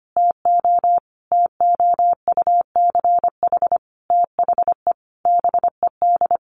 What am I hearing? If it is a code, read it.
Morse code, heard as a keyed tone, about 25 wpm, 700 Hz. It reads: TO TOUCH THE BED
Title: Sentences from Top 1200 Words 25wpm